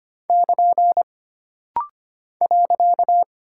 Morse code, heard as a keyed tone, about 25 wpm, 700 Hz. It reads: <KN> .
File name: Single Punctuation-Prosign - Rapid-Fire - 25wpm